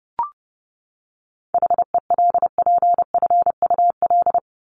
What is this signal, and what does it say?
Morse code, heard as a keyed tone, about 30 wpm, 700 Hz. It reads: HELPFUL
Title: Top 751-1000 Words 30wpm